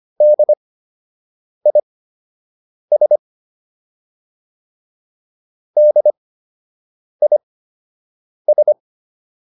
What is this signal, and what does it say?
Morse code, heard as a keyed tone, about 25 wpm, spaced as for 7 wpm, 600 Hz. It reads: DIS DIS